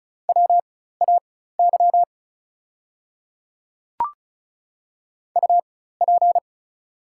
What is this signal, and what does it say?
Morse code, heard as a keyed tone, about 34 wpm, spaced as for 16 wpm, 690 Hz. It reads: WAY UP
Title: 4x Character Spacing - Top 100 Words - 35wpm